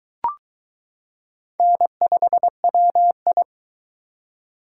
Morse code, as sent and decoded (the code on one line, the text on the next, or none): -. ..... .-- ..
N5WI